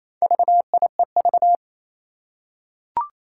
Morse code, heard as a keyed tone, about 29 wpm, 700 Hz. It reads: VIEV
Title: Four Challenging Letter-Number Combinations 28wpm